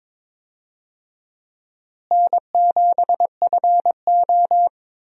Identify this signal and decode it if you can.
Morse code, heard as a keyed tone, about 21 wpm, 700 Hz. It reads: N7FO
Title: USA Call Signs - 4 Characters - Repeated 3x - 22wpm